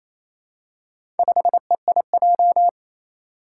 Morse code, heard as a keyed tone, about 28 wpm, 700 Hz. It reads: 5EIJ
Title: Four Challenging Letter-Number Combinations 28wpm